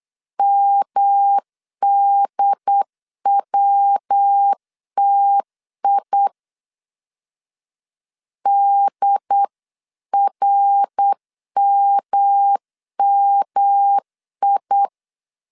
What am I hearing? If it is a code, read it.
Morse code, heard as a keyed tone, about 8 wpm, 800 Hz. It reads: MDWTI DRMMI